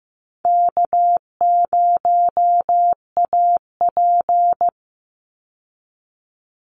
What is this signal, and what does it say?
Morse code, heard as a keyed tone, about 15 wpm, 700 Hz. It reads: K0AP
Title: Easy Call Signs Repeated 2x - 15wpm